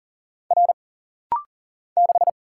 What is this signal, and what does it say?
Morse code, heard as a keyed tone, about 40 wpm, 700 Hz. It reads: R 6
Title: Single Letter-Number - Rapid-Fire 40wpm